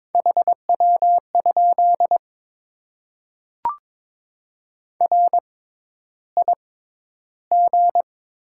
Morse code, heard as a keyed tone, about 22 wpm, spaced as for 11 wpm, 700 Hz. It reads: HW? RIG
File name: Top 20 QSO Elements - 6x Character Spacing - 22wpm